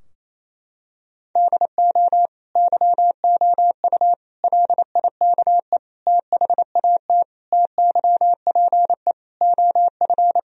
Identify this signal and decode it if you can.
Morse code, heard as a keyed tone, about 28 wpm, 700 Hz. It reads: DO YOU LIKE THAT TYPE OF
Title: Sentences from Top 1000 Words - Review Part 3 - 28wpm